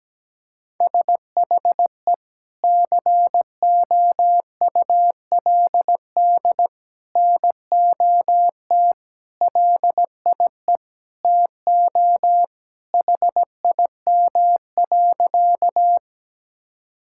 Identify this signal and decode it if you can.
Morse code, heard as a keyed tone, about 17 wpm, 700 Hz. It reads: SHE COULD NOT LIE TO HIM.
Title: Sentences from Top 1600 Words 17wpm